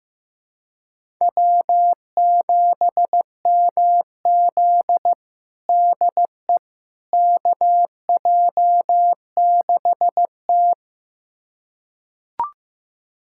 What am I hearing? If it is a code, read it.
Morse code, heard as a keyed tone, about 15 wpm, 700 Hz. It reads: W7MZ DE KJ6T
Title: USA Callsign DE Callsign - 4 Characters - 15wpm